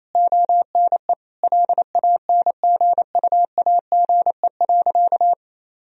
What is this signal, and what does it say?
Morse code, heard as a keyed tone, about 28 wpm, 700 Hz. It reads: ONE LANGUAGE.
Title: Sentences from Top 1000 Words - Review Part 3 - 28wpm